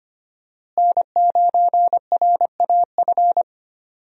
Morse code, heard as a keyed tone, about 25 wpm, 700 Hz. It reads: N9RAF
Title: USA Call Signs - 5 Characters - Repeated 2x - 25wpm